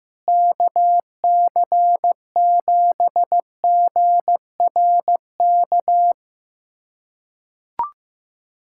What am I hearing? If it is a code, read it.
Morse code, heard as a keyed tone, about 15 wpm, 700 Hz. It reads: KC7GRK
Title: USA Callsign DE Callsign - 6 Characters - 15wpm